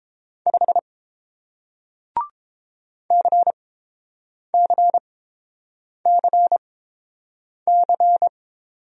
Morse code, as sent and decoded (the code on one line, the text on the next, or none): ..... / -.-. -.-. -.-. -.-.
5 CCCC